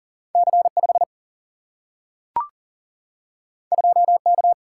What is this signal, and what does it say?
Morse code, heard as a keyed tone, about 40 wpm, 700 Hz. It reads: C5 2K
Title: Single-Letter Single-Number 40wpm